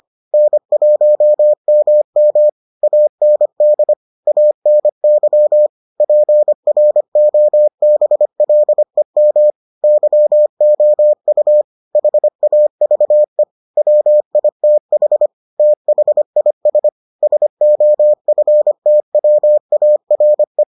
Morse code, heard as a keyed tone, about 25 wpm, 600 Hz. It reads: N1MM AND ANY PROBLEM YOU HAVE WITH THIS SOFTWARE